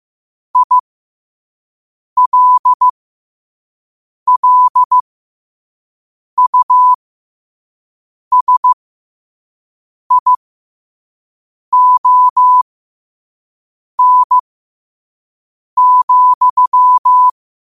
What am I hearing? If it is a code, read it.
Morse code, heard as a keyed tone, about 15 wpm, spaced as for 5 wpm, 1000 Hz. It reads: ILLUSION,
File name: qotdfiles / 5wpm